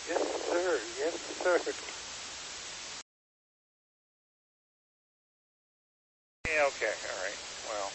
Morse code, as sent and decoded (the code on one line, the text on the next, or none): none